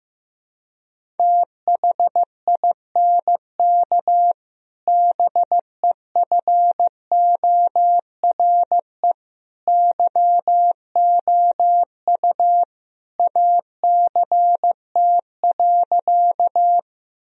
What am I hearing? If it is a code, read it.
Morse code, heard as a keyed tone, about 15 wpm, 700 Hz. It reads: THINK BEFORE YOU ACT.
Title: Sentences from Top 1000 Words - Review Part 1 - 15wpm